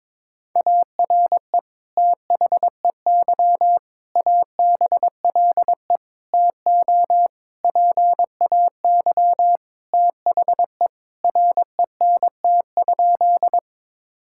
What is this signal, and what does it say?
Morse code, heard as a keyed tone, about 22 wpm, 700 Hz. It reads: ARE THEY ABLE TO PAY THE RENT?
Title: Sentences from Top 1000 Words - Review Part 5 - 22wpm